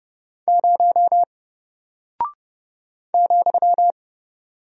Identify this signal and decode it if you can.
Morse code, heard as a keyed tone, about 30 wpm, 700 Hz. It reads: 0 ,
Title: Single Character (Letter:Number:Punctuation:Prosign) Alt 30wpm